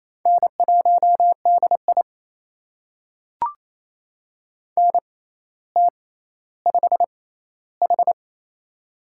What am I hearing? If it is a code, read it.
Morse code, heard as a keyed tone, about 28 wpm, spaced as for 14 wpm, 700 Hz. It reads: N1DI NT5H
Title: USA Call Signs - 4 Characters - 6x Character Spacing - 28wpm